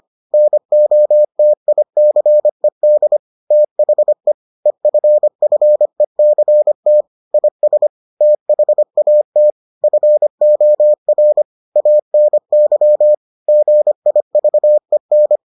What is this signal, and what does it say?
Morse code, heard as a keyed tone, about 25 wpm, 600 Hz. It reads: NOTICED THE EFFECT IS THAT FOR ANY GIVEN